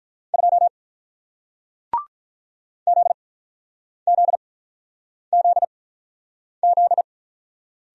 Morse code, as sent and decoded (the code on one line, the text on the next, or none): ..--- / --.. --.. --.. --..
2 ZZZZ